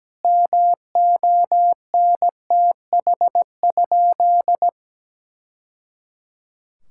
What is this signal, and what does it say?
Morse code, heard as a keyed tone, about 17 wpm, 700 Hz. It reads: MONTH?
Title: Sentences from Top 1000 Words - Review Part 5 - 17wpm